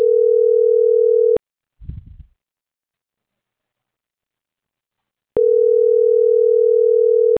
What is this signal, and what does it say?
Ringback tone (North America), 440 + 480 Hz.